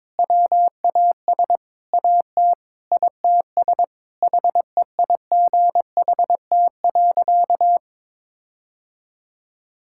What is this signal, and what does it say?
Morse code, heard as a keyed tone, about 21 wpm, 700 Hz. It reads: WAS AT ITS HEIGHT.
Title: Sentences from Top 1600 Words 22wpm